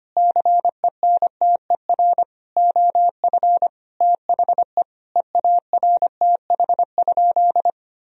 Morse code, heard as a keyed tone, about 25 wpm, 700 Hz. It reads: CENTER OF THE EARTH?
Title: Sentences from Top 1000 Words - Review Part 2 - 25wpm